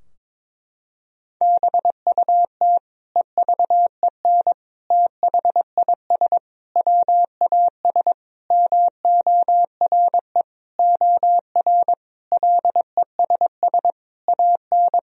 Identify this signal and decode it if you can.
Morse code, heard as a keyed tone, about 22 wpm, 700 Hz. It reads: BUT EVEN THIS WAS MORE OR LESS AN
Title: Sentences from Top 1000 Words - Review Part 2 - 22wpm